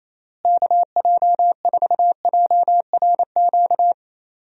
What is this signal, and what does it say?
Morse code, heard as a keyed tone, about 28 wpm, 700 Hz. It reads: KJ4JRQ
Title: USA Call Signs - 6 Characters - Repeated 2x - 28wpm